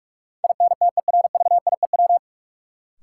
Morse code, heard as a keyed tone, about 42 wpm, 700 Hz. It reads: INTERVIEW
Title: Sentences from Top 1600 Words - New Word Drill Alt - 45wpm